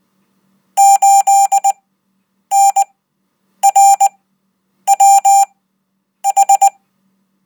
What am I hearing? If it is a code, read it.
Morse code, heard as a keyed tone, about 20 wpm, spaced as for 8 wpm, 790 Hz. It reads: 8NRWH